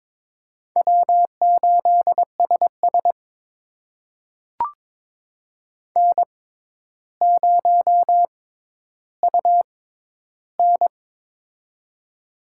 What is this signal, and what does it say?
Morse code, heard as a keyed tone, about 22 wpm, spaced as for 11 wpm, 700 Hz. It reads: W8SS N0UN
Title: USA Call Signs - 4 Characters - 6x Character Spacing - 22wpm